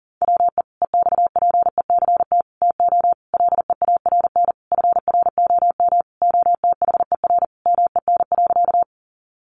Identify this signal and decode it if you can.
Morse code, heard as a keyed tone, about 40 wpm, 700 Hz. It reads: WE EXPECT TO LEARN FROM OTHER MEN.
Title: Sentences from Top 1100 Words 40wpm